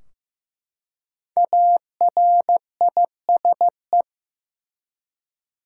Morse code, heard as a keyed tone, about 15 wpm, 700 Hz. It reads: ARISE
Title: Five Letter Words - 15wpm